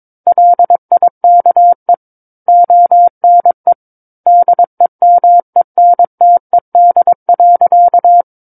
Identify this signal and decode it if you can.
Morse code, heard as a keyed tone, about 22 wpm, 700 Hz. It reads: LIKE ONE DEMENTED.
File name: Book One / 13-Chapter 13 - 22 WPM